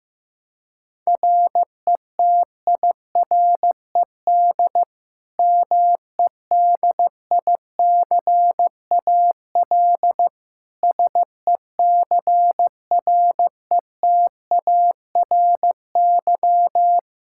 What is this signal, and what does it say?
Morse code, heard as a keyed tone, about 15 wpm, 700 Hz. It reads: RETIRED MEDICAL SECRETARY
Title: Practice QSO - Exchange 3 - More Info - 15wpm